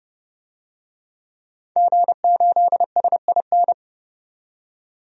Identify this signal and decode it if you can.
Morse code, heard as a keyed tone, about 29 wpm, 700 Hz. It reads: G8SIN
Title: DXCC-100 Call Signs - 5 Characters - 30wpm